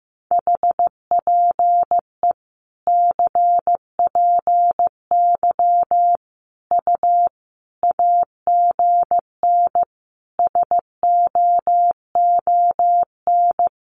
Morse code, heard as a keyed tone, about 15 wpm, 700 Hz. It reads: HPE CPY U AGN SOON
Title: Common QSO Phrases 15wpm